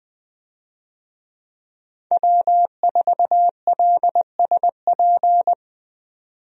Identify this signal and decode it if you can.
Morse code, heard as a keyed tone, about 20 wpm, 700 Hz. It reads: W4LSP
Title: USA Call Signs - 5 Characters - 20wpm